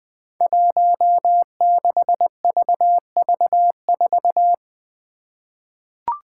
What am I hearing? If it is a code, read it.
Morse code, heard as a keyed tone, about 20 wpm, 700 Hz. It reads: 16VV4
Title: Five Challenging Letter-Number Combinations 20wpm